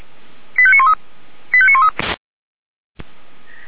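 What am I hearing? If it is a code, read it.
Five-tone selective call (ZVEI), heard as quick runs of five tones, tones of about 70 ms. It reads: 85913 85913
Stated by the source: ZVEI five-tone call received by SDR, from feuerwehr-orlamuende.org (Feuerwehr Orlamünde) archive.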